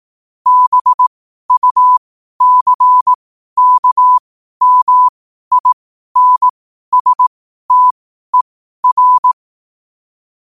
Morse code, heard as a keyed tone, about 18 wpm, spaced as for 12 wpm, 1000 Hz. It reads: BUCKMINSTER
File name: qotdfiles / 12wpm